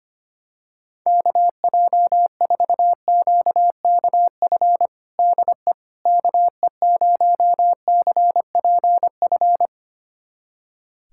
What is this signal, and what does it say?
Morse code, heard as a keyed tone, about 25 wpm, 700 Hz. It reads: KJ4QKF DE KE0CPF
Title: USA Callsign DE Callsign - 6 Characters - 25wpm